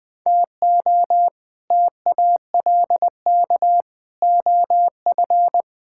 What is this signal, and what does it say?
Morse code, heard as a keyed tone, about 20 wpm, 700 Hz. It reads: TO TALK OF
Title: Build A Phrase - 5-Word Phrases - 20wpm